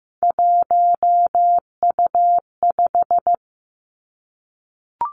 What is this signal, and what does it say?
Morse code, heard as a keyed tone, about 15 wpm, 700 Hz. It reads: 1U5
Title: Three Challenging Letter-Number Combinations 15wpm